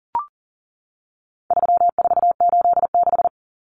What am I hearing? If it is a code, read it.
Morse code, heard as a keyed tone, about 40 wpm, 700 Hz. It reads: 3486
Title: Common Cut Numbers - 4-Digit 40wpm